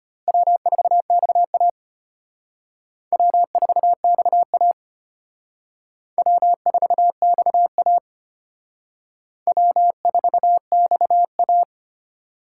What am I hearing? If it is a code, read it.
Morse code, heard as a keyed tone, about 32 wpm, 700 Hz. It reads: W4XA W4XA W4XA W4XA